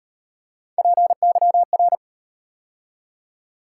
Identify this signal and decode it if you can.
Morse code, heard as a keyed tone, about 37 wpm, 700 Hz. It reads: PYR